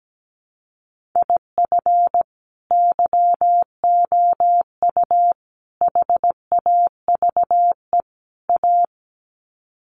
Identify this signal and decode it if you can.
Morse code, heard as a keyed tone, about 17 wpm, 700 Hz. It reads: IF YOU HAVE A
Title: Sets of 4 Words 17wpm